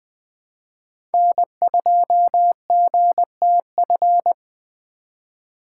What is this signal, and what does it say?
Morse code, heard as a keyed tone, about 20 wpm, 700 Hz. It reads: N2GTF
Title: USA Call Signs - 5 Characters - 4x Character Spacing - 20wpm